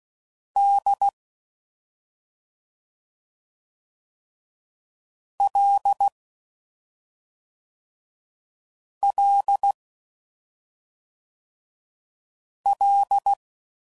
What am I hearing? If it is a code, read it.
Morse code, heard as a keyed tone, about 16 wpm, 800 Hz. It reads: D L L L
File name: Prefixes / FLs & THs / Single L's